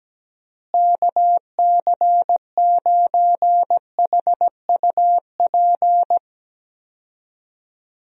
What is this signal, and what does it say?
Morse code, heard as a keyed tone, about 17 wpm, 700 Hz. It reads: KC9HUP
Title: USA Call Signs - 6 Characters - Repeated 3x - 17wpm